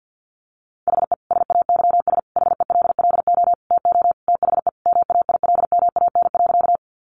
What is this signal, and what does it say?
Morse code, heard as a keyed tone, about 50 wpm, 700 Hz. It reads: HE SAYS HELLO TO THE MAILMAN.
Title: Easy Phrases 50wpm